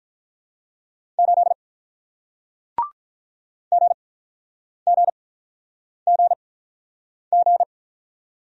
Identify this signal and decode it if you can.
Morse code, heard as a keyed tone, about 48 wpm, spaced as for 9 wpm, 690 Hz. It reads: 8 GGGG